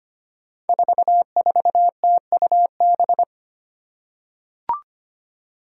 Morse code, heard as a keyed tone, about 25 wpm, 700 Hz. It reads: 44TUB
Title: Five Challenging Letter-Number Combinations 25wpm